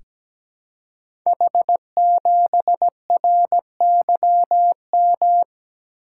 Morse code, heard as a keyed tone, about 17 wpm, 700 Hz. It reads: H7RYM
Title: Code Groups - 5 Letter:Numbers - 17wpm